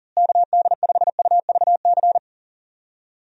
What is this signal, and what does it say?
Morse code, heard as a keyed tone, about 40 wpm, 700 Hz. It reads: KD5UVC
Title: USA Call Signs - 6 Characters - Repeated 3x - 40wpm